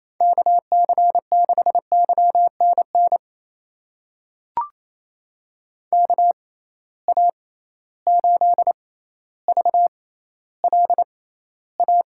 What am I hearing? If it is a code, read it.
Morse code, heard as a keyed tone, about 28 wpm, spaced as for 14 wpm, 700 Hz. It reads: KC6YNN KA8VLA
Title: USA Call Signs - 6 Characters - 6x Character Spacing - 28wpm